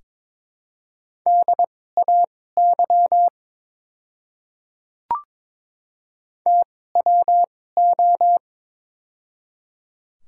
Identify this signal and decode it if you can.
Morse code, heard as a keyed tone, about 22 wpm, spaced as for 16 wpm, 700 Hz. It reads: DAY TWO
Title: 2x Character Spacing - Top 100 Words - 22wpm